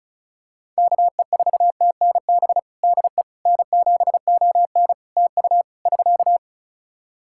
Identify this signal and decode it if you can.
Morse code, heard as a keyed tone, about 36 wpm, 700 Hz. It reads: KE4TNB DE N7ON TU <SK>
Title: Practice QSO - Exchange 4 - Ending - 35wpm